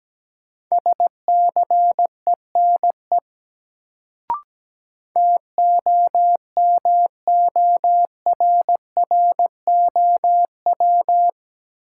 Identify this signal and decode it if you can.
Morse code, heard as a keyed tone, about 17 wpm, 700 Hz. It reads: SCENE TOMORROW
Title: Sentences from Top 1100 Words - New Word Drill - 17wpm